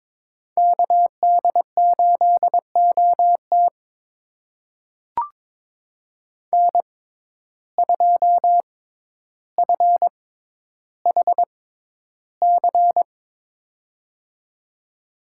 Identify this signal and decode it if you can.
Morse code, heard as a keyed tone, about 22 wpm, spaced as for 11 wpm, 700 Hz. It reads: KD8OT N2FHC
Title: USA Call Signs - 5 Characters - 6x Character Spacing - 22wpm